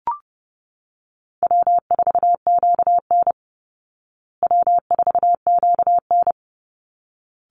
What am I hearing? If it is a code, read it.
Morse code, heard as a keyed tone, about 29 wpm, 700 Hz. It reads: W4QN W4QN
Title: Easy Call Signs Repeated 2x - 30wpm